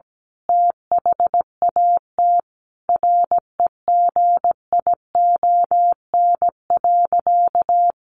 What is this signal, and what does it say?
Morse code, heard as a keyed tone, about 17 wpm, 700 Hz. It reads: THAT REGION.
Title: A Sound Of Thunder 17wpm